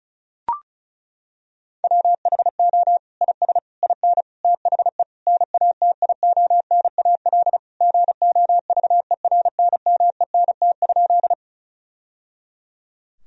Morse code, heard as a keyed tone, about 36 wpm, 700 Hz. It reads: WHO IS IN THE NATIONAL GOVERNMENT?
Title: Sentences from Top 1000 Words - Review Part 4 - 35wpm